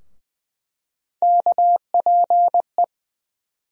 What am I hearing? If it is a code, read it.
Morse code, heard as a keyed tone, about 20 wpm, 700 Hz. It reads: KPE
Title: USA Call Sign Postfixes - 3 Letters 20wpm